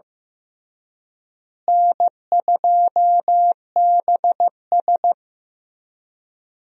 Morse code, heard as a keyed tone, about 15 wpm, 700 Hz. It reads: N2BS
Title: USA Call Signs - 4 Characters - Repeated 3x - 15wpm